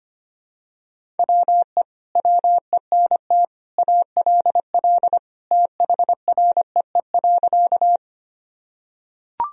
Morse code, heard as a keyed tone, about 25 wpm, 700 Hz. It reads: WE WENT ALL THREE.
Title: Sentences from Top 1000 Words - Review Part 1 - 25wpm